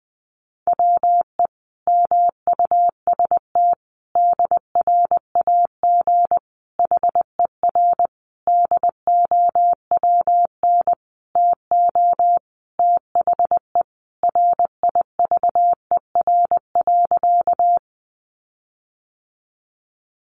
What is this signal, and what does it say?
Morse code, heard as a keyed tone, about 20 wpm, 700 Hz. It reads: WE MUST DRAG HER DOWN TO THE RIVER.
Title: Sentences from Top 1500 Words 20wpm